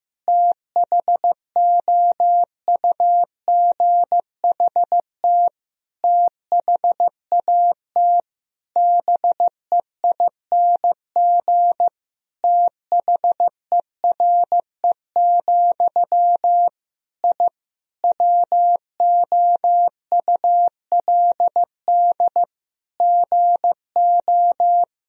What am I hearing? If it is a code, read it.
Morse code, heard as a keyed tone, about 15 wpm, 700 Hz. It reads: THOUGHT THAT BEING THERE, I WOULD GO
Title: Sentences from Top 1000 Words - Review Part 3 - 15wpm